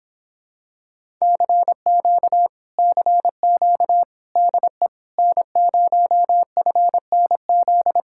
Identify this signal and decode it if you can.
Morse code, heard as a keyed tone, about 26 wpm, 700 Hz. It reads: CQ CQ DE N0FNZ